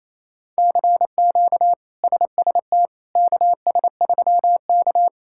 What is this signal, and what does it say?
Morse code, heard as a keyed tone, about 28 wpm, 700 Hz. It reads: CQ SST KS3K
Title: K1USN SST QSO Practice - 28wpm